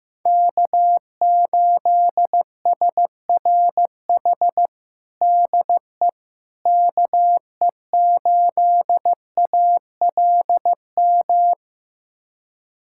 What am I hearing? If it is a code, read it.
Morse code, heard as a keyed tone, about 15 wpm, 700 Hz. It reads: K8SRH DE KE8ALM
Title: USA Callsign DE Callsign 15wpm